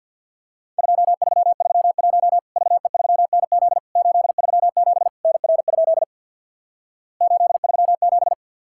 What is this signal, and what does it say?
Morse code, heard as a keyed tone, about 50 wpm, 700 Hz. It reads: 2331 VE3NZ 837 NR? 837
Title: Worked All Europe DX Contest - QTC Exchange with Repeats - 50wpm